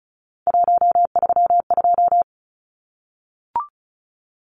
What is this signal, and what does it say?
Morse code, heard as a keyed tone, about 34 wpm, 700 Hz. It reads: 132
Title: Numbers - 3-Digit 35wpm